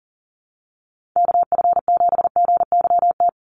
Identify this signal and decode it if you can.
Morse code, heard as a keyed tone, about 40 wpm, 700 Hz. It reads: KF7GYT
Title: USA Call Signs 40wpm